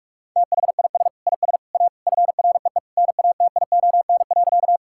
Morse code, heard as a keyed tone, about 46 wpm, 700 Hz. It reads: THIS IS A FREE NATION.
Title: Sentences from Top 1000 Words - Review Part 6 - 45wpm